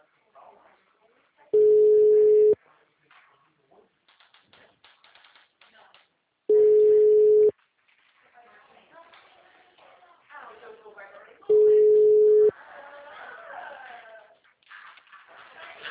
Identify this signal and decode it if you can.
Ringback tone (Europe, 425 Hz), with speech.
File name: out- -2022- - - .52373